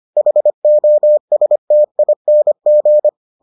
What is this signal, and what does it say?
Morse code, heard as a keyed tone, about 25 wpm, 600 Hz. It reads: HOSTING